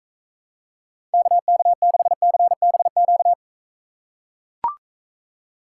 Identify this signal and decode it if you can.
Morse code, heard as a keyed tone, about 42 wpm, 700 Hz. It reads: KK6CBQ